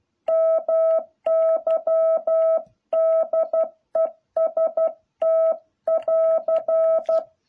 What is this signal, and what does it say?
Morse code, heard as a keyed tone, about 12 wpm, 650 Hz. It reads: MY DEST <AR>
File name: jh8cbh / 2021cw